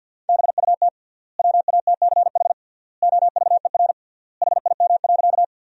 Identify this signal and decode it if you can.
Morse code, heard as a keyed tone, about 51 wpm, 700 Hz. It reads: BUT WATCH OVER HIM.
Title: 3x Word-Spacing - Sentences from Top 300 Words - 50wpm